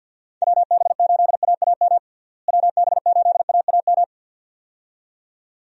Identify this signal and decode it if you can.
Morse code, heard as a keyed tone, about 50 wpm, 700 Hz. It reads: WB8AAM WB8AAM
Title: USA Call Signs - 6 Characters - Repeated 2x - 50wpm